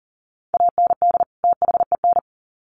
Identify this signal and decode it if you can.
Morse code, heard as a keyed tone, about 40 wpm, 700 Hz. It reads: AND THEN
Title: Sets of 2 Words 40wpm